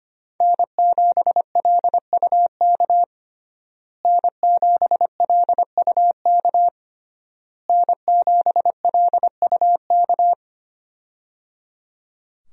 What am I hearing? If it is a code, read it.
Morse code, heard as a keyed tone, about 25 wpm, 700 Hz. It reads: N7LUK N7LUK N7LUK
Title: USA Call Signs - 5 Characters - Repeated 3x - 25wpm